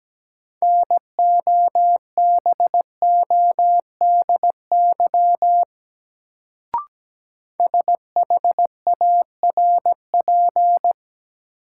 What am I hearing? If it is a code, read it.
Morse code, heard as a keyed tone, about 17 wpm, 700 Hz. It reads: NOBODY SHARP
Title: Sentences from Top 1600 Words - New Word Drill - 17wpm